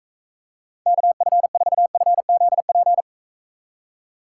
Morse code, heard as a keyed tone, about 43 wpm, 700 Hz. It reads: KF4FZP